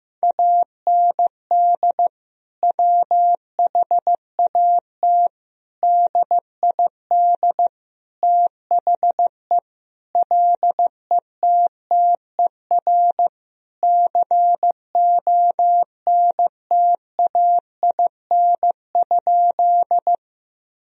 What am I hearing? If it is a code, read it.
Morse code, heard as a keyed tone, about 15 wpm, 700 Hz. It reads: AND WHAT DID THE LETTER CONTAIN?
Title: Sentences from Top 1000 Words - Review Part 2 - 15wpm